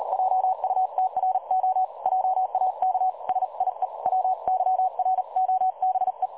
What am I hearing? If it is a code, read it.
Morse code, heard as a keyed tone, about 38 wpm, 750 Hz. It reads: JUTRO JAK SIE WYROBE
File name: dlazaawansowanych / nagraniazpasma / websdr recording 3565.0kHz